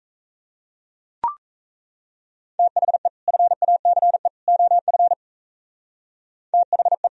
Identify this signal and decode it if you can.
Morse code, heard as a keyed tone, about 37 wpm, 700 Hz. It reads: THE FACE OF THE